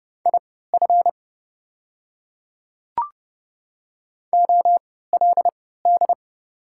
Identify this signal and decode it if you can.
Morse code, heard as a keyed tone, about 31 wpm, spaced as for 17 wpm, 700 Hz. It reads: IF OLD
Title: 3x Character Spacing - Top 200 Words - 30wpm